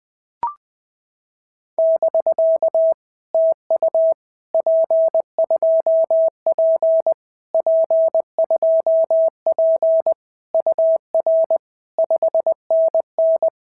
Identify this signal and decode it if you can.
Morse code, heard as a keyed tone, about 20 wpm, 650 Hz. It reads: <BK> TU P2P P2P UR 5NN
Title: POTA QSO Practice - 20wpm